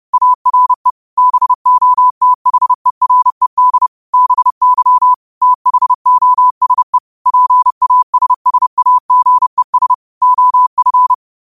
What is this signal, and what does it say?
Morse code, heard as a keyed tone, about 30 wpm, 1000 Hz. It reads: ARE BOTHERED BY THOSE PASSAGES OF
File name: qotdfiles / 30wpm